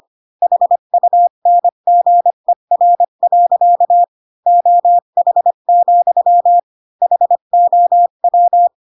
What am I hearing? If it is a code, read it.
Morse code, heard as a keyed tone, about 25 wpm, 700 Hz. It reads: HUNGER. OH, HOW
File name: Pinocchio / Pinocchio ch5 20 25 ik1hge